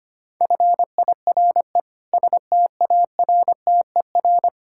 Morse code, heard as a keyed tone, about 25 wpm, 700 Hz. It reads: FIRE STARTER
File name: practice / 25wpm / Difficult Words - Speed-Racing with Hints - 25wpm